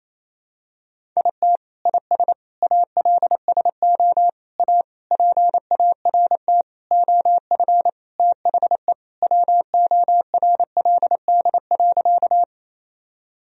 Morse code, heard as a keyed tone, about 28 wpm, 700 Hz. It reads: IT IS ALSO A PART OF THE WORLD.
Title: Sentences from Top 1000 Words - Review Part 1 - 28wpm